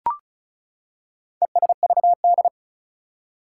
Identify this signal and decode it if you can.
Morse code, heard as a keyed tone, about 36 wpm, 700 Hz. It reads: ESVD
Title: Four Challenging Letter-Number Combinations 35wpm